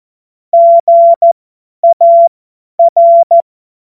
Morse code, heard as a keyed tone, about 14 wpm, spaced as for 10 wpm, 680 Hz. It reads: GAR
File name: Morsekurs / Witze / Witz 27. April 2021, 50 F70